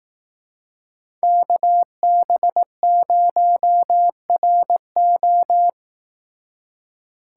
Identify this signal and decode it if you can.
Morse code, heard as a keyed tone, about 18 wpm, 700 Hz. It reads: KB0RO
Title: Speed Racing - USA 2x2 Call Signs 15wpm